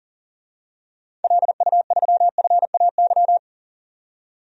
Morse code, heard as a keyed tone, about 40 wpm, 700 Hz. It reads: LU3FAY
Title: DXCC-100 Call Signs - 6 Characters - 40wpm